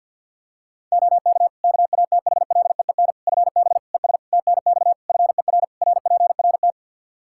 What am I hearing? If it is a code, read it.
Morse code, heard as a keyed tone, about 52 wpm, 700 Hz. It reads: OK KATHLEEN FB ES TNX FER RPRT
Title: Practice QSO - Exchange 2 - Info - 50wpm